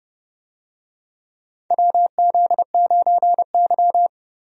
Morse code, heard as a keyed tone, about 29 wpm, 700 Hz. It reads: WZ9Y
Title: USA Call Signs - 4 Characters - 30wpm